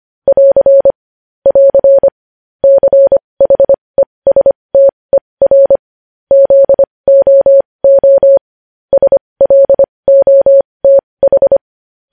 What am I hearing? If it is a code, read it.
Morse code, heard as a keyed tone, about 25 wpm, spaced as for 20 wpm, 550 Hz. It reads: <AR> <AR> CHESTER ZOO SLOTH
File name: podcast / News headlines-1x-25x20